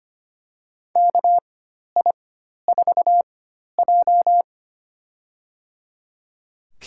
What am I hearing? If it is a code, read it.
Morse code, heard as a keyed tone, about 25 wpm, spaced as for 12 wpm, 700 Hz. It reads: KI4J
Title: USA Call Signs - 4 Characters - 4x Character Spacing - 25wpm